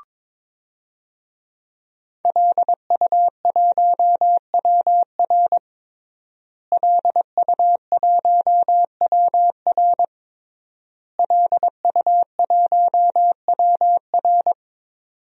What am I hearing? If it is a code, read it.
Morse code, heard as a keyed tone, about 22 wpm, 700 Hz. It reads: LU1WR LU1WR LU1WR
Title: DXCC-100 Call Signs - 5 Characters - Repeated 3x - 22wpm